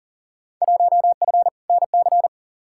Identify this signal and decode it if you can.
Morse code, heard as a keyed tone, about 40 wpm, 700 Hz. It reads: 1F NC
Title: Field Day Exchanges - 40wpm